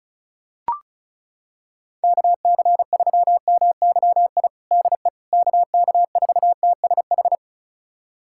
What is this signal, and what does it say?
Morse code, heard as a keyed tone, about 34 wpm, 700 Hz. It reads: KC3MYI DE KK4TSH
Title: USA Callsign DE Callsign - 6 Characters - 35wpm